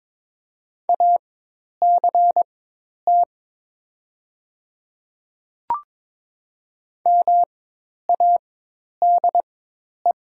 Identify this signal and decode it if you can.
Morse code, heard as a keyed tone, about 22 wpm, spaced as for 10 wpm, 700 Hz. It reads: ACT MADE
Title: 4x Character Spacing - Top 300 Words - 22wpm